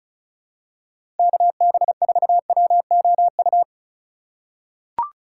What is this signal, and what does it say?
Morse code, heard as a keyed tone, about 36 wpm, 700 Hz. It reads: KB4WOU
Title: USA Call Signs - 6 Characters - 4x Character Spacing - 35wpm